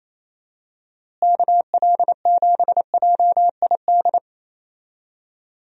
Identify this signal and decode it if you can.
Morse code, heard as a keyed tone, about 28 wpm, 700 Hz. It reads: KL7JID
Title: DXCC-100 Call Signs - 6 Characters - Repeated 2x - 28wpm